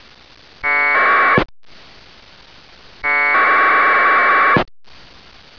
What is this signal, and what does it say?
Packet radio (AX.25, as in APRS).